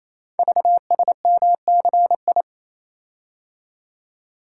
Morse code, heard as a keyed tone, about 27 wpm, 700 Hz. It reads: VSMCI
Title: Code Groups - 5 Letter:Numbers - 28wpm